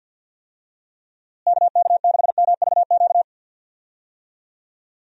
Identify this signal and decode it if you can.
Morse code, heard as a keyed tone, about 50 wpm, 700 Hz. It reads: KK6MVQ
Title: USA Call Signs - 6 Characters - Repeated 3x - 50wpm